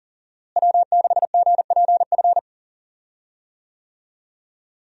Morse code, heard as a keyed tone, about 40 wpm, 700 Hz. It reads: W6GPF
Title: USA Call Signs - 5 Characters - Repeated 3x - 40wpm